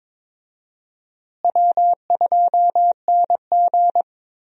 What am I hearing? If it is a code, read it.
Morse code, heard as a keyed tone, about 22 wpm, 700 Hz. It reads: W2NG
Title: USA Call Signs - 4 Characters - Repeated 3x - 22wpm